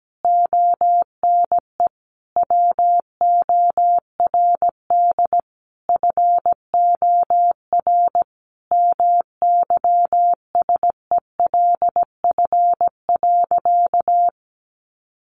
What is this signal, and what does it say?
Morse code, heard as a keyed tone, about 17 wpm, 700 Hz. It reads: ONE WORD FOR MYSELF.
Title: Sentences from Top 1300 Words 17wpm